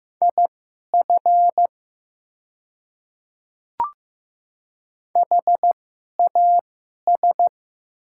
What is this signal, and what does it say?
Morse code, heard as a keyed tone, about 15 wpm, spaced as for 11 wpm, 700 Hz. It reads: IF HAS
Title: 2x Character Spacing - Top 300 Words - 15wpm